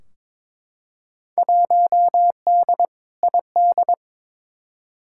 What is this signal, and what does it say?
Morse code, heard as a keyed tone, about 22 wpm, 700 Hz. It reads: 1D ID
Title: Field Day Exchanges - 22wpm